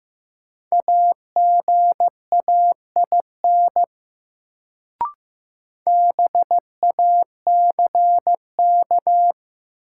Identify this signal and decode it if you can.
Morse code, heard as a keyed tone, about 16 wpm, 700 Hz. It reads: AGAIN BACK
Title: Sentences from Top 200 Words - New Word Drill - 15wpm